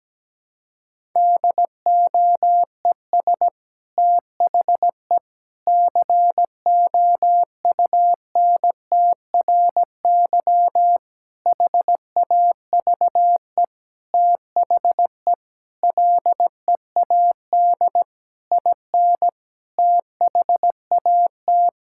Morse code, heard as a keyed tone, about 17 wpm, 700 Hz. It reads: DOES THE COUNTRY HAVE THE LEAD IN THAT
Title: Sentences from Top 1000 Words - Review Part 5 - 17wpm